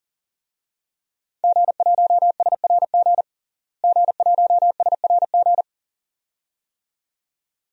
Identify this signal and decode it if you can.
Morse code, heard as a keyed tone, about 40 wpm, 700 Hz. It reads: G1SRG G1SRG
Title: DXCC-100 Call Signs - 5 Characters - Repeated 2x - 40wpm